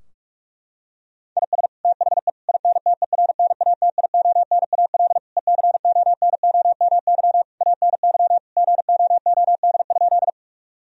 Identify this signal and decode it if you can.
Morse code, heard as a keyed tone, about 46 wpm, 700 Hz. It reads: IS THE INTERNATIONAL ECONOMY ANY GOOD?
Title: Sentences from Top 1000 Words - Review Part 4 - 45wpm